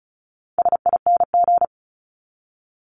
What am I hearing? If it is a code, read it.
Morse code, heard as a keyed tone, about 34 wpm, 700 Hz. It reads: SING
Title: Top 500 Words 35wpm